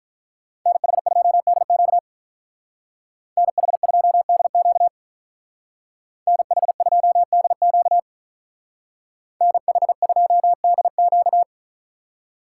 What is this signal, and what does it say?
Morse code, heard as a keyed tone, about 43 wpm, 700 Hz. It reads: NH2DQ NH2DQ NH2DQ NH2DQ
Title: Speed Racing - USA Call Signs 35wpm